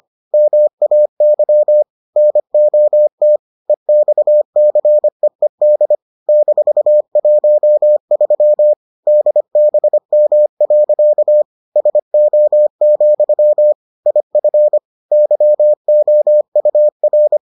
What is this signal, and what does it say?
Morse code, heard as a keyed tone, about 25 wpm, 600 Hz. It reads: MAY NOT EXCEED -13 DBM. SO, IF YOUR